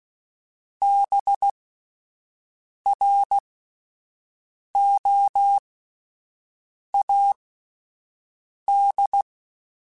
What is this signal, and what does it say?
Morse code, heard as a keyed tone, about 16 wpm, spaced as for 5 wpm, 800 Hz. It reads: BROAD
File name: Spaced / Chuckles Spaced-mp3s / MiddleAge sp @10